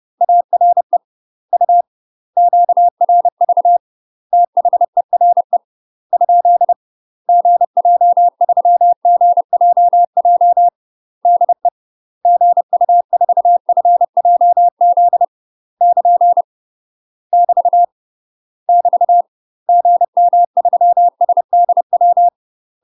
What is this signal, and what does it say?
Morse code, heard as a keyed tone, about 30 wpm, 700 Hz. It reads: ARE U QRV THERE ? GJ3GJJ DE GU4FJZ <KN> <BT> <BT> GM3SDW
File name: UK MASTER QSOs TRAINING / 30WPM / UK MASTER QSOs 0018